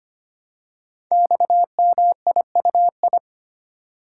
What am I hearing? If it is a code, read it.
Morse code, heard as a keyed tone, about 25 wpm, 700 Hz. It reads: XMIUI
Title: Code Groups - 5 Letters - 25wpm